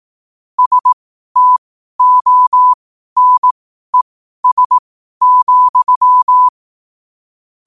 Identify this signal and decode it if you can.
Morse code, heard as a keyed tone, about 18 wpm, spaced as for 12 wpm, 1000 Hz. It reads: STONES,
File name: qotdfiles / 12wpm